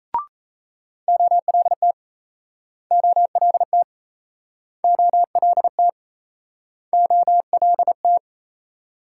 Morse code, heard as a keyed tone, about 38 wpm, 700 Hz. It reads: OLT OLT OLT OLT